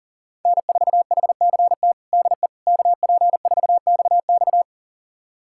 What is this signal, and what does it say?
Morse code, heard as a keyed tone, about 40 wpm, 700 Hz. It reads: N4HCT DE KP4XX
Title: USA Callsign DE Callsign - 5 Characters - 40wpm